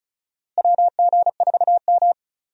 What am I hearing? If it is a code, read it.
Morse code, heard as a keyed tone, about 36 wpm, 700 Hz. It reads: WG4M
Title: USA Call Signs - 4 Characters - Repeated 3x - 35wpm